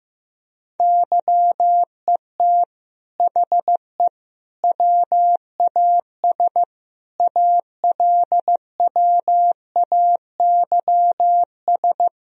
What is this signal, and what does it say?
Morse code, heard as a keyed tone, about 15 wpm, 700 Hz. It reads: YET HE WAS ALWAYS
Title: Sentences from Top 1000 Words - Review Part 3 - 15wpm